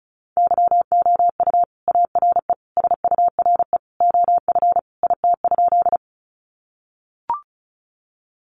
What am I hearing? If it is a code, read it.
Morse code, heard as a keyed tone, about 34 wpm, 700 Hz. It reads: YOU ARE SURE OF IT?
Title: Sentences from Top 300 Words 35wpm